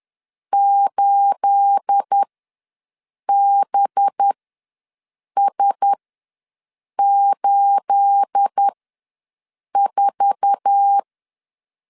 Morse code, heard as a keyed tone, about 10 wpm, spaced as for 6 wpm, 800 Hz. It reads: 8BS84
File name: LEZIONI / 9 28 ws 10wpm 4124